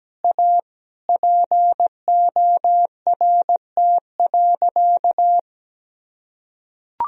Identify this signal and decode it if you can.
Morse code, heard as a keyed tone, about 17 wpm, 700 Hz. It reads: A PORT.
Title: Sentences from Top 1000 Words - Review Part 1 - 17wpm